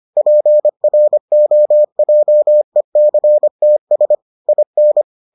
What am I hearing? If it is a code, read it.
Morse code, heard as a keyed tone, about 25 wpm, 600 Hz. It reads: PROJECTS IN